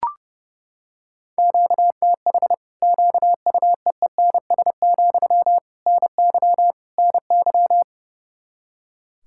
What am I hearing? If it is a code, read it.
Morse code, heard as a keyed tone, about 30 wpm, 700 Hz. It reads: QTH QUEENS, NY NY
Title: Practice QSO - Exchange 1 - Beginning - 30wpm